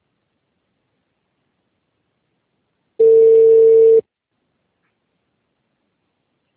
Ringback tone (China, 450 Hz).